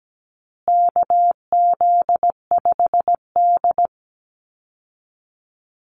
Morse code, heard as a keyed tone, about 17 wpm, 700 Hz. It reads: KZ5D
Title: Call Signs Repeated 2x - 17wpm